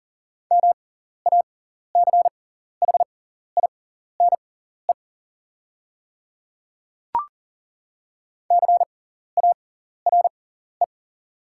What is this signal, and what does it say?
Morse code, heard as a keyed tone, about 40 wpm, spaced as for 14 wpm, 690 Hz. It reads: MACHINE CARE
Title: 6x Character Spacing - Top 500 Words - 40wpm